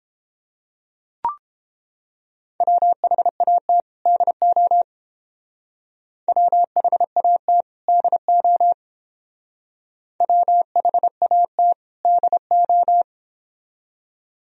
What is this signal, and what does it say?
Morse code, heard as a keyed tone, about 29 wpm, 700 Hz. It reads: WHAT DO WHAT DO WHAT DO